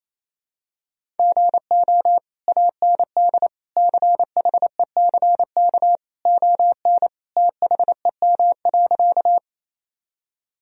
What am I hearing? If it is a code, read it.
Morse code, heard as a keyed tone, about 28 wpm, 700 Hz. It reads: GO AND CHECK ON THEM.
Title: Sentences from Top 1000 Words - Review Part 3 - 28wpm